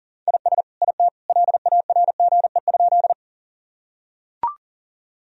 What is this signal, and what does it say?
Morse code, heard as a keyed tone, about 40 wpm, 700 Hz. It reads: IS IT LARGE?
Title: Sentences from Top 1000 Words - Review Part 1 - 40wpm